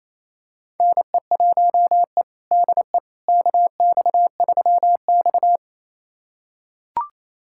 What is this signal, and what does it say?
Morse code, heard as a keyed tone, about 28 wpm, 700 Hz. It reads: NE1E DE KX3X
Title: USA Callsign DE Callsign - 4 Characters - 28wpm